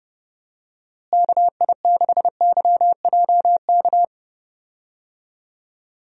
Morse code, heard as a keyed tone, about 31 wpm, 700 Hz. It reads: KI6YJK
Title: USA Call Signs - 6 Characters - Repeated 3x - 30wpm